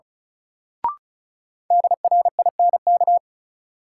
Morse code, heard as a keyed tone, about 36 wpm, 700 Hz. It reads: DRINK
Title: Sentences from Top 1500 Words - New Word Drill Alt - 35wpm